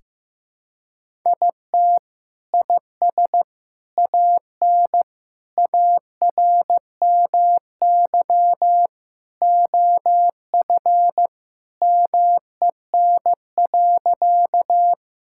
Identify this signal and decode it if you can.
Morse code, heard as a keyed tone, about 15 wpm, 700 Hz. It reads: IT IS AN ARMY OF MEN.
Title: Sentences from Top 1000 Words - Review Part 5 - 15wpm